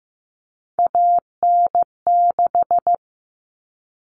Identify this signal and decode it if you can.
Morse code, heard as a keyed tone, about 15 wpm, 700 Hz. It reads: AN6
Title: Common Cut Numbers - 3-Digit 15wpm